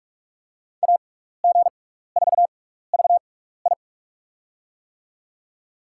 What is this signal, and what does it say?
Morse code, heard as a keyed tone, about 46 wpm, spaced as for 15 wpm, 700 Hz. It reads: AG4VI
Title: USA Call Signs - 5 Characters - 6x Character Spacing - 45wpm